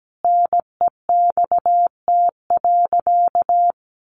Morse code, heard as a keyed tone, about 17 wpm, 700 Hz. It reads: NEXT.
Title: Sentences from Top 1500 Words 17wpm